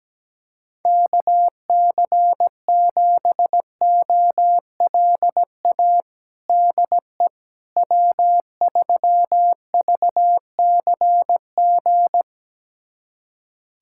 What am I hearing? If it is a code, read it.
Morse code, heard as a keyed tone, about 17 wpm, 700 Hz. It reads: KC7OLA DE W3VCG
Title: USA Callsign DE Callsign 17wpm